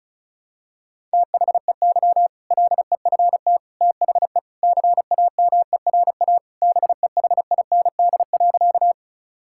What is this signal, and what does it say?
Morse code, heard as a keyed tone, about 36 wpm, 690 Hz. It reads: THEY LEFT THE CAMERA BEHIND.
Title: Sentences from Top 1000 Words - Review Part 5 - 35wpm